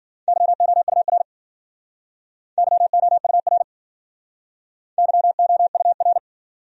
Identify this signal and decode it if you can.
Morse code, heard as a keyed tone, about 50 wpm, 700 Hz. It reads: YOUR YOUR YOUR
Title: Speed-Racing Top 200 Words 40wpm